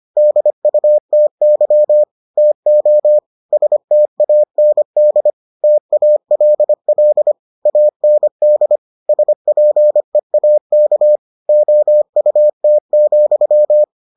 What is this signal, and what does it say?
Morse code, heard as a keyed tone, about 25 wpm, 600 Hz. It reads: DUTY TO STAND TALL AND SPEAK OUT,